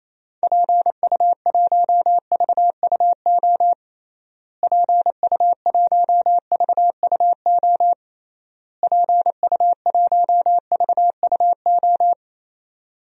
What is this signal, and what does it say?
Morse code, heard as a keyed tone, about 28 wpm, 700 Hz. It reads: PU1VUO PU1VUO PU1VUO
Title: DXCC-100 Call Signs - 6 Characters - Repeated 3x - 28wpm